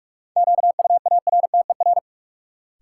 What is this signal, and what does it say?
Morse code, heard as a keyed tone, about 46 wpm, 700 Hz. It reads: QUARTER
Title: Sentences from Top 1300 Words - New Word Drill Alt - 45wpm